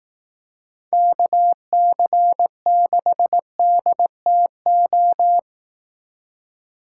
Morse code, heard as a keyed tone, about 18 wpm, 700 Hz. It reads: KC6DTO
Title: Speed Racing - USA 2x3 Call Signs 15wpm